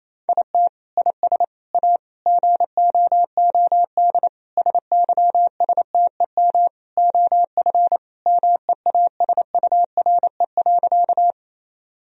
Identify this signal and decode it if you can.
Morse code, heard as a keyed tone, about 28 wpm, 700 Hz. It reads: IT IS A GOOD SYSTEM OF MEASURE.
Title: Sentences from Top 1000 Words - Review Part 3 - 28wpm